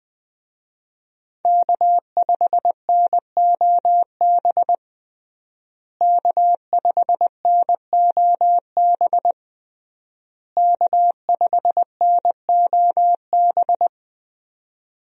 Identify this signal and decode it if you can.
Morse code, heard as a keyed tone, about 20 wpm, 700 Hz. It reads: K5NOB K5NOB K5NOB
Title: USA Call Signs - 5 Characters - Repeated 3x - 20wpm